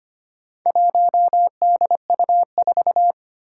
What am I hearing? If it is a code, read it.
Morse code, heard as a keyed tone, about 25 wpm, 700 Hz. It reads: 1DU4
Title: Four Challenging Letter-Number Combinations 25wpm